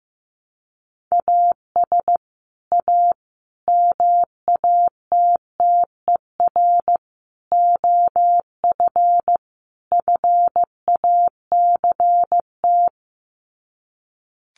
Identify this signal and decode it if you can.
Morse code, heard as a keyed tone, about 15 wpm, 700 Hz. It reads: AS A MATTER OF FACT
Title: Sets of 5 Words 15wpm